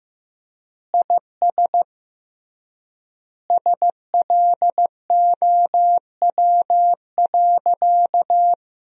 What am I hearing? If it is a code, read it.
Morse code, heard as a keyed tone, about 15 wpm, 700 Hz. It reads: IS SLOW.
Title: 3x Word-Spacing - Sentences from Top 400 Words - 15wpm